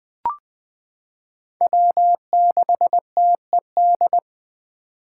Morse code, heard as a keyed tone, about 20 wpm, 700 Hz. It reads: W6TED
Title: 2023 SOTA Chaser Call Signs - 20wpm